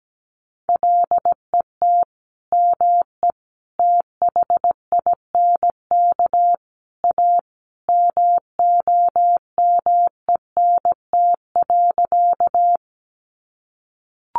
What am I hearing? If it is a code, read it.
Morse code, heard as a keyed tone, about 17 wpm, 700 Hz. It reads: LET ME THINK A MOMENT.
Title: Sentences from Top 1000 Words 17wpm